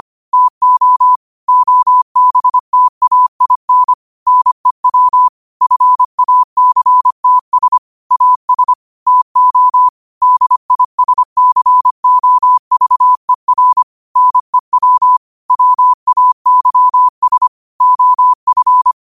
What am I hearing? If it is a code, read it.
Morse code, heard as a keyed tone, about 25 wpm, 1000 Hz. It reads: TO OBTAIN NEW FACTS AS TO DISCOVER NEW WAYS OF
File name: 25wpm